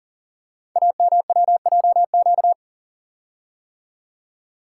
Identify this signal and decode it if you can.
Morse code, heard as a keyed tone, about 43 wpm, 700 Hz. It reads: AMWJQ
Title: Code Groups - 5 Letters - 40wpm